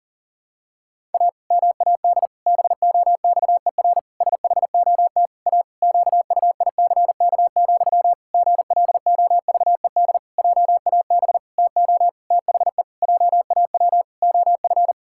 Morse code, heard as a keyed tone, about 40 wpm, 700 Hz. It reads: A MAD BOXER SHOT A QUICK, GLOVED JAB TO THE JAW OF
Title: Pangram Challenge v2 - 40wpm